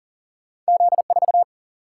Morse code, heard as a keyed tone, about 40 wpm, 700 Hz. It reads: Z4
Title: Single-Letter Single-Number 40wpm